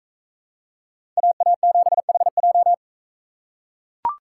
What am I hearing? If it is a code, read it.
Morse code, heard as a keyed tone, about 44 wpm, 700 Hz. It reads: AA7HJ